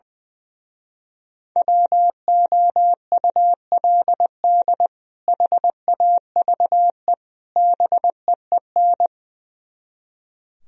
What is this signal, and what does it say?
Morse code, heard as a keyed tone, about 21 wpm, 700 Hz. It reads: WOULD HAVE BEEN
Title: Speed-Racing - Sets of 3 Words 20wpm